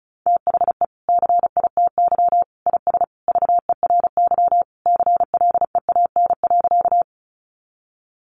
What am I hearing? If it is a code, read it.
Morse code, heard as a keyed tone, about 34 wpm, 700 Hz. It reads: THE CITY IS VERY CLEAN.
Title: Easy Phrases 35wpm